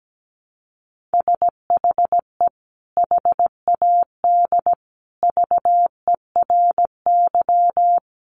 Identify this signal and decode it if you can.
Morse code, heard as a keyed tone, about 17 wpm, 700 Hz. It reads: SHE HAD VERY
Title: Sentences from Top 400 Words 17wpm